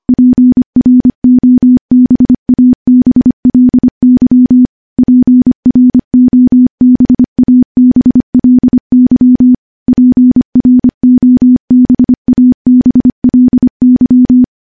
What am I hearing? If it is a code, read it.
Morse code, heard as a keyed tone, about 25 wpm, 250 Hz. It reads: PROBABLY PROBABLY PROBABLY
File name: morse-training-25WPM